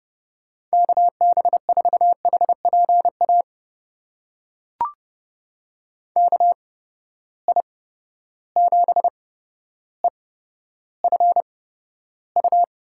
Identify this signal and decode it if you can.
Morse code, heard as a keyed tone, about 32 wpm, spaced as for 13 wpm, 700 Hz. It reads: KB4HPA KI7EFU
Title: USA Call Signs - 6 Characters - 8x Character Spacing - 30wpm